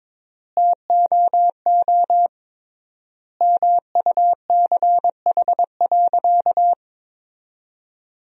Morse code, heard as a keyed tone, about 21 wpm, 700 Hz. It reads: TOO MUCH.
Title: 3x Word-Spacing - Sentences from Top 500 Words - 22wpm